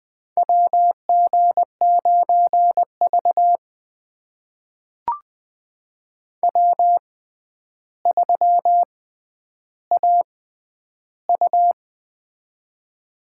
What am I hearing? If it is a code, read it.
Morse code, heard as a keyed tone, about 20 wpm, spaced as for 10 wpm, 700 Hz. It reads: WG9V W3AU
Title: USA Call Signs - 4 Characters - 6x Character Spacing - 20wpm